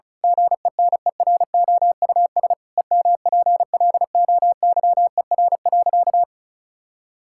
Morse code, heard as a keyed tone, about 36 wpm, 700 Hz. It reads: GENEROUS EMPLOYER.
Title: Pangram Challenge v2 - 35wpm